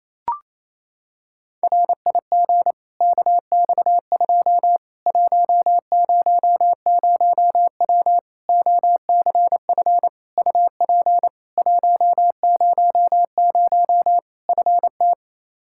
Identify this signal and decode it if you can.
Morse code, heard as a keyed tone, about 28 wpm, 700 Hz. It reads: RIG KX2 100W OCF UP 100 FT
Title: Practice QSO - Exchange 2 - Info - 28wpm